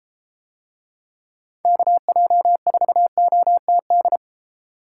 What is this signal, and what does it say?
Morse code, heard as a keyed tone, about 32 wpm, 690 Hz. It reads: KJ4OTD